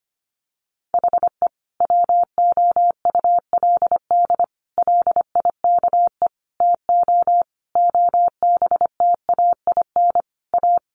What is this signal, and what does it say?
Morse code, heard as a keyed tone, about 25 wpm, 700 Hz. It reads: HE WOULD LIKE TO OBTAIN A
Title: Sentences from Top 1300 Words 25wpm